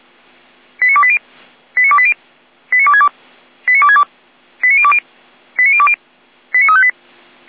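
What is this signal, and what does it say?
Five-tone selective call (ZVEI), heard as quick runs of five tones, tones of about 70 ms. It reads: 89280 89280 89262 89262 89020 89020 89358
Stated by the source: ZVEI five-tone selective call received on SDR.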